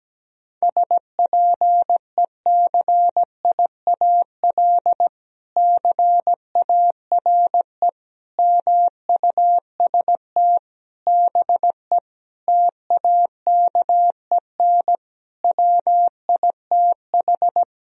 Morse code, heard as a keyed tone, about 17 wpm, 700 Hz. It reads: SPECIAL CARE MUST BE TAKEN WITH
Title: Sentences from Top 1000 Words - Review Part 4 - 17wpm